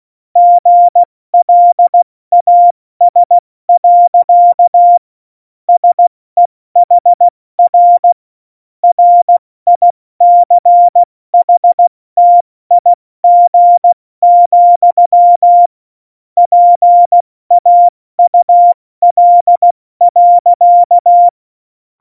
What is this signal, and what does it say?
Morse code, heard as a keyed tone, about 16 wpm, 700 Hz. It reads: GLAS. SEHR RICHTIG, PAUL.
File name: Morsekurs / Witze / Witz 02. April 2019, 70 F80